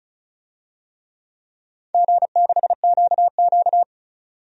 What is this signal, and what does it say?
Morse code, heard as a keyed tone, about 34 wpm, 700 Hz. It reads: G6QQ
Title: DXCC-100 Call Signs - 4 Characters - 35wpm